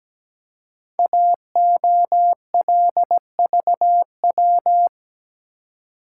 Morse code, heard as a keyed tone, about 17 wpm, 700 Hz. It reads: AOLVW
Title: Code Groups - 5 Letters - 17wpm